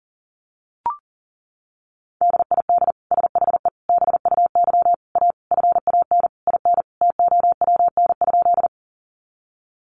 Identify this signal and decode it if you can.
Morse code, heard as a keyed tone, about 40 wpm, 700 Hz. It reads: DID SHE BUY A FAN IN TOWN?
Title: Sentences from Top 1300 Words 40wpm